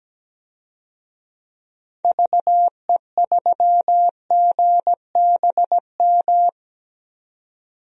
Morse code, heard as a keyed tone, about 17 wpm, 700 Hz. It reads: VE3GBM
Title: DXCC-100 Call Signs - 6 Characters - 17wpm